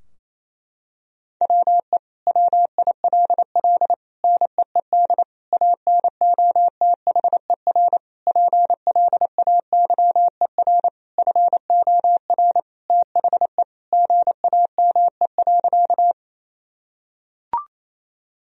Morse code, heard as a keyed tone, about 28 wpm, 700 Hz. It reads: WE WILL NEED ANOTHER PLAYER FOR THE GAME.
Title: Sentences from Top 1000 Words - Review Part 4 - 28wpm